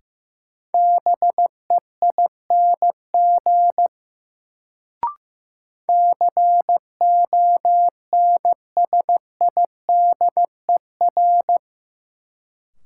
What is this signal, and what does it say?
Morse code, heard as a keyed tone, about 15 wpm, 700 Hz. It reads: BEING CONSIDER
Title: Sentences from Top 600 Words - New Word Drill - 15wpm